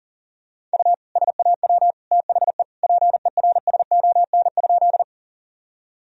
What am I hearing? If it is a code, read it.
Morse code, heard as a keyed tone, about 40 wpm, 700 Hz. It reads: U SAW THE PERSON?
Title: Sentences from Top 400 Words and Abbreviations 40wpm